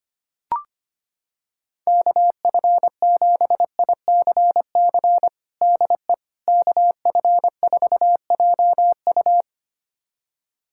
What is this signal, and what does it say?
Morse code, heard as a keyed tone, about 25 wpm, 700 Hz. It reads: KF7ICC DE KF4JU
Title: USA Callsign DE Callsign 25wpm